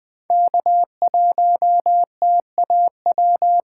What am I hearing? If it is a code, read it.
Morse code, heard as a keyed tone, about 20 wpm, 700 Hz. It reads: K1TAW
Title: USA Call Signs - 5 Characters - 8x Character Spacing - 20wpm